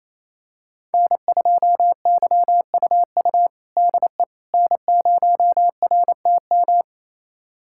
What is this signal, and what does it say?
Morse code, heard as a keyed tone, about 28 wpm, 700 Hz. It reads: N2YUU DE N0RTM
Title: USA Callsign DE Callsign - 5 Characters - 28wpm